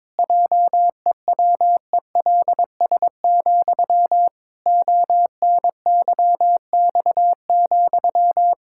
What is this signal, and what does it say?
Morse code, heard as a keyed tone, about 22 wpm, 700 Hz. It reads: JEWELS, ONYX,
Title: Pangram Challenge v2 - 22wpm